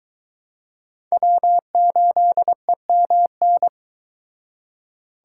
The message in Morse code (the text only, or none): W8EMN